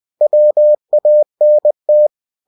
Morse code, heard as a keyed tone, about 20 wpm, 600 Hz. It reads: WANT
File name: cw-books / winnie-the-pooh / winnie-20wpm-page-0032